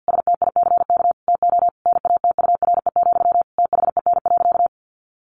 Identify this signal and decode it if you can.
Morse code, heard as a keyed tone, about 48 wpm, 700 Hz. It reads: STICK TO NATURE, THEN.
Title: Sentences from Top 1200 Words 50wpm